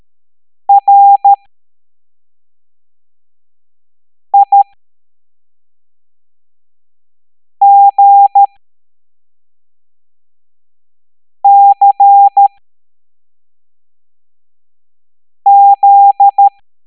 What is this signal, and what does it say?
Morse code, heard as a keyed tone, about 13 wpm, spaced as for 3 wpm, 800 Hz. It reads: RIGCZ